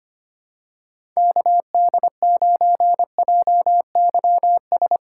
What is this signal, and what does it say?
Morse code, heard as a keyed tone, about 25 wpm, 700 Hz. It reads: KD9JYS
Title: USA Call Signs - 6 Characters - 3x Character Spacing - 25wpm